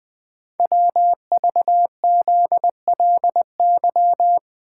Morse code, heard as a keyed tone, about 20 wpm, 700 Hz. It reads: WVZLY
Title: Code Groups - 5 Letters - 20wpm